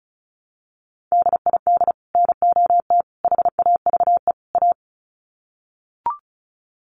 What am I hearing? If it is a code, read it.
Morse code, heard as a keyed tone, about 34 wpm, 700 Hz. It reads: DID NOT HAVE A
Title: Sets of 4 Words 35wpm